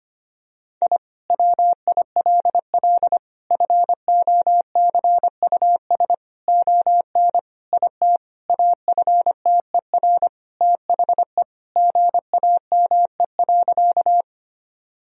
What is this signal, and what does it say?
Morse code, heard as a keyed tone, about 25 wpm, 700 Hz. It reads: I WILL FOCUS ON IT AFTER THE GAME.
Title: Sentences from Top 1000 Words - Review Part 4 - 25wpm